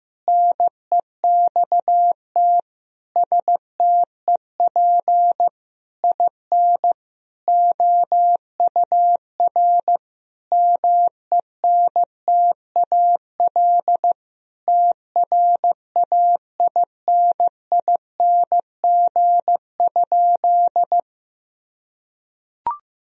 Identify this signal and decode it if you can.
Morse code, heard as a keyed tone, about 15 wpm, 700 Hz. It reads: NEXT STEP IN OUR MENTAL TRAINING?
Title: Sentences from Top 1000 Words - Review Part 4 - 15wpm